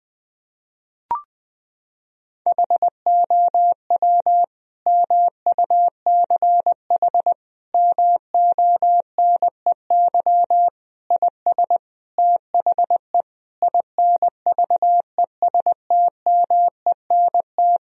Morse code, heard as a keyed tone, about 20 wpm, 700 Hz. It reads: HOW MUCH MONEY IS THE INVESTMENT
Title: Sentences from Top 1000 Words - Review Part 4 - 20wpm